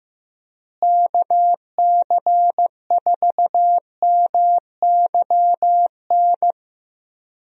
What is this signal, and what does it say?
Morse code, heard as a keyed tone, about 16 wpm, 700 Hz. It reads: KC4MYN
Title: USA Call Signs - 6 Characters - Repeated 2x - 15wpm